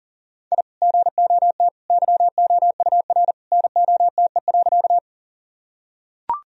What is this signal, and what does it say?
Morse code, heard as a keyed tone, about 40 wpm, 700 Hz. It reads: I GOT YOUR NOTE.
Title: Sentences from Top 1000 Words - Review Part 2 - 40wpm